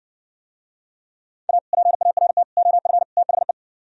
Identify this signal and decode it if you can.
Morse code, heard as a keyed tone, about 59 wpm, 700 Hz. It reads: A PART OF THE